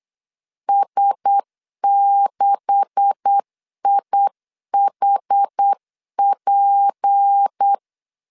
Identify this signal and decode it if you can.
Morse code, heard as a keyed tone, about 8 wpm, 800 Hz. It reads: S6IHP
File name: CW / DET / 1 9 1500C S A 8wpm 5637